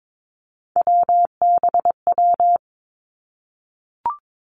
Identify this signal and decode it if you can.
Morse code, heard as a keyed tone, about 22 wpm, 700 Hz. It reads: WBW
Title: Three Challenging Letter-Number Combinations 22wpm